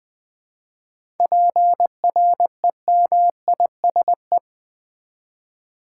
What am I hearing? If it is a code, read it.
Morse code, heard as a keyed tone, about 20 wpm, 700 Hz. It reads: PREMISE
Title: 7 Letter Words - First 3 Letters Given - 20wpm